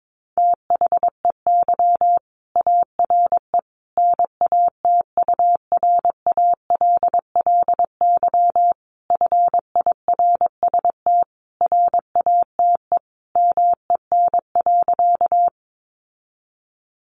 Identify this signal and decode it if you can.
Morse code, heard as a keyed tone, about 22 wpm, 700 Hz. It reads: THEY ARE NATURALLY FIRST RATE MEN.
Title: Sentences from Top 1300 Words 22wpm